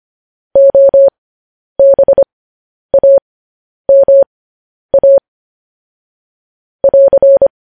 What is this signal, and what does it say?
Morse code, heard as a keyed tone, about 25 wpm, spaced as for 10 wpm, 550 Hz. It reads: OBAMA <AR>
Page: news headlines in morse code at 10 wpm